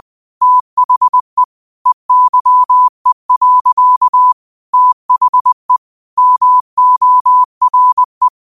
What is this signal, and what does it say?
Morse code, heard as a keyed tone, about 20 wpm, 1000 Hz. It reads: THE EYE. THE MORE
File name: qotdfiles / 20wpm